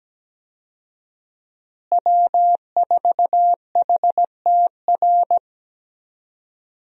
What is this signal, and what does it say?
Morse code, heard as a keyed tone, about 17 wpm, 700 Hz. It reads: W4HTR
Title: USA Call Signs - 5 Characters - Repeated 3x - 17wpm